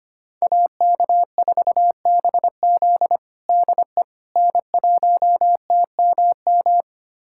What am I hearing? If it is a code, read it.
Morse code, heard as a keyed tone, about 25 wpm, 700 Hz. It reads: AK4BZ DE N1TMM
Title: USA Callsign DE Callsign - 5 Characters - 25wpm